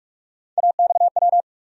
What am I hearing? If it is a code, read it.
Morse code, heard as a keyed tone, about 46 wpm, 700 Hz. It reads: AXW
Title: USA Call Sign Postfixes - 3 Letters 45wpm